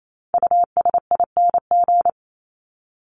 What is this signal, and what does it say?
Morse code, heard as a keyed tone, about 28 wpm, 700 Hz. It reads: USING
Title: ING Suffix - Top 100 Words with Base Words 28wpm